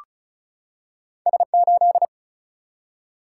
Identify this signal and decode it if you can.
Morse code, heard as a keyed tone, about 35 wpm, 700 Hz. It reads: S8
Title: Single-Letter Single-Number 35wpm